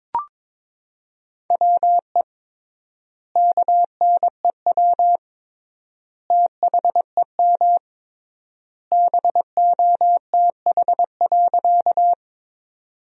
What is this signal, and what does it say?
Morse code, heard as a keyed tone, about 21 wpm, 700 Hz. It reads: WE KNEW THEM BOTH.
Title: 3x Word-Spacing - Sentences from Top 400 Words - 22wpm